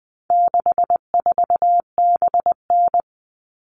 Morse code, heard as a keyed tone, about 20 wpm, 700 Hz. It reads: 64BN
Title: All Cut Numbers - 4-Digit 20wpm